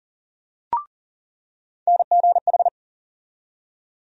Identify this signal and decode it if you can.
Morse code, heard as a keyed tone, about 40 wpm, 700 Hz. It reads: NGH
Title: USA Call Sign Postfixes - 3 Letters 40wpm